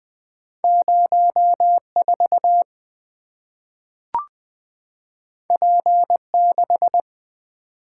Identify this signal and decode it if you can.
Morse code, heard as a keyed tone, about 20 wpm, 700 Hz. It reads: 04 P6
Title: Two Letter-Number Combinations - 20wpm